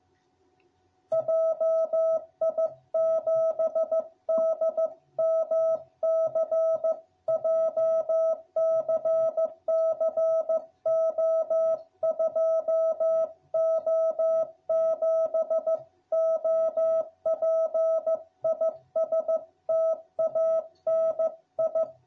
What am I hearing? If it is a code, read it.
Morse code, heard as a keyed tone, about 15 wpm, 660 Hz. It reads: JI7DMC JCC O2O7 OP IS TANI